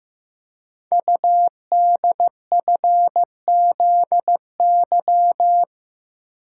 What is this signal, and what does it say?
Morse code, heard as a keyed tone, about 15 wpm, 700 Hz. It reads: UDFZY
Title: Code Groups - 5 Letters - 15wpm